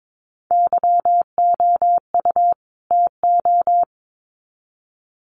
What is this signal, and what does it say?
Morse code, heard as a keyed tone, about 22 wpm, 700 Hz. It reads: YOU TO
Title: Sets of 2 Words 22wpm